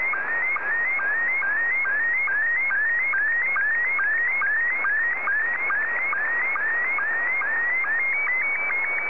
SSTV (slow-scan television), Scottie 1.
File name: sstv / S1-VK2ELO